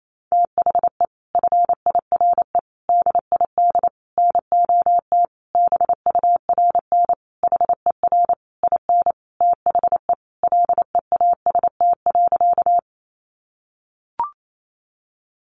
Morse code, heard as a keyed tone, about 28 wpm, 700 Hz. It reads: THE FIRE DID NOT BURN HER IN THE LEAST.
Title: Sentences from Top 1300 Words 28wpm